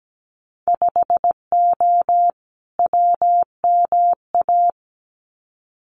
Morse code, heard as a keyed tone, about 17 wpm, 700 Hz. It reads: 5O WMA
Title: Winter Field Day Exchanges 17wpm